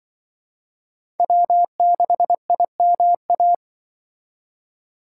Morse code, heard as a keyed tone, about 25 wpm, 700 Hz. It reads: W6IMA
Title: Speed Racing - USA 1x3 Call Signs 20wpm